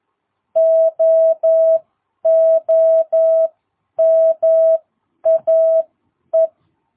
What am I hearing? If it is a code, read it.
Morse code, heard as a keyed tone, about 11 wpm, spaced as for 9 wpm, 660 Hz. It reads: OOMAE
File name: jh8cbh / 2022cw